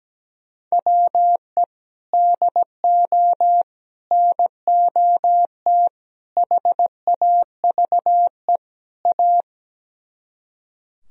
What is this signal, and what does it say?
Morse code, heard as a keyed tone, about 17 wpm, 700 Hz. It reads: WE DO NOT HAVE A
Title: Speed-Racing - Sets of 5 Words 17wpm